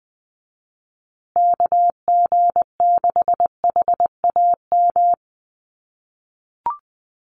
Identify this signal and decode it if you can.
Morse code, heard as a keyed tone, about 20 wpm, 700 Hz. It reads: KG6HAM
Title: USA Call Signs Repeated 2x 20wpm